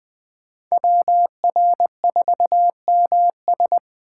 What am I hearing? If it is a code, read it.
Morse code, heard as a keyed tone, about 20 wpm, 700 Hz. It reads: WR4MS
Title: USA Call Signs - 5 Characters - 3x Character Spacing - 20wpm